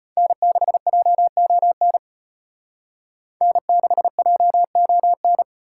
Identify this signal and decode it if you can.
Morse code, heard as a keyed tone, about 32 wpm, 700 Hz. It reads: N6JON N6JON